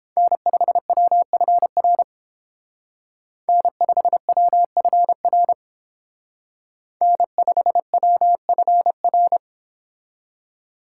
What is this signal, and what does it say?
Morse code, heard as a keyed tone, about 31 wpm, 700 Hz. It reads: N5WFR N5WFR N5WFR